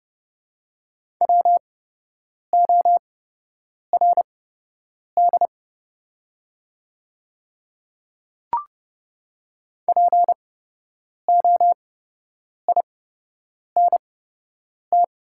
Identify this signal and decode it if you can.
Morse code, heard as a keyed tone, about 30 wpm, spaced as for 8 wpm, 700 Hz. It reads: WORD POINT
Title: 8x Character Spacing - Top 200 Words - 30wpm